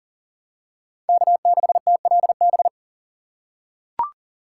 Morse code, heard as a keyed tone, about 40 wpm, 700 Hz. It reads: K6TLB
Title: USA Call Signs - 4x Character Spacing - 40wpm